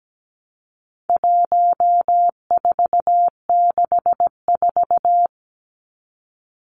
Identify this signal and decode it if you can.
Morse code, heard as a keyed tone, about 17 wpm, 700 Hz. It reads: 1464
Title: Numbers - 4-Digit 17wpm